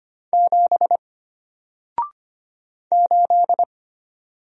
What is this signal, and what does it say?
Morse code, heard as a keyed tone, about 25 wpm, 700 Hz. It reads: 7 8
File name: Single-Letter-Number-25wpm